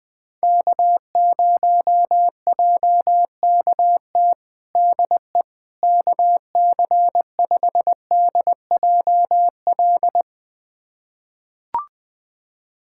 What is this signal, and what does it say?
Morse code, heard as a keyed tone, about 21 wpm, 700 Hz. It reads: K0JKT DE KC5DJL
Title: USA Callsign DE Callsign 20wpm